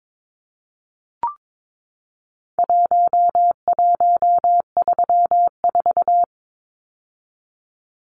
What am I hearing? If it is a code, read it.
Morse code, heard as a keyed tone, about 22 wpm, 700 Hz. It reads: 1134
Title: Numbers - Contest Style - 4-Digit 22wpm